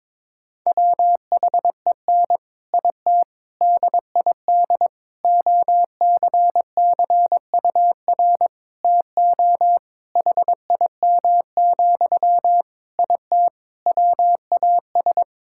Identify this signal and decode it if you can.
Morse code, heard as a keyed tone, about 22 wpm, 700 Hz. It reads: WHEN IT DID OCCUR TO HIM, IT WAS
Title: Sentences from Top 1000 Words - Review Part 6 - 22wpm